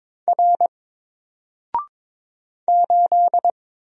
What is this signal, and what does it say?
Morse code, heard as a keyed tone, about 22 wpm, 700 Hz. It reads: R 8
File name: Single-Letter-Number-22wpm